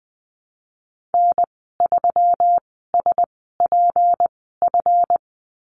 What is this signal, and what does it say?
Morse code, heard as a keyed tone, about 20 wpm, spaced as for 15 wpm, 700 Hz. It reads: N3SPF
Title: 161 Lesson 26g - Letter g - Call Signs 20wpm